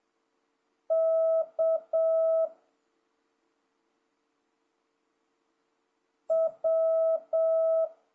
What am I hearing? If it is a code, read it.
Morse code, heard as a keyed tone, about 7 wpm, 650 Hz. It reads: K W
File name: jh8cbh / 2026cw